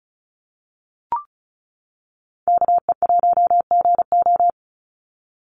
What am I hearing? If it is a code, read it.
Morse code, heard as a keyed tone, about 35 wpm, 700 Hz. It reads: KE1GO
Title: USA Call Signs 35wpm